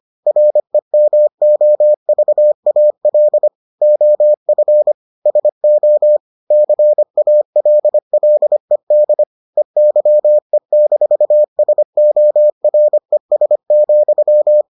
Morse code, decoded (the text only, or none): REMOVAL OF SO CALLED EYE-SORES,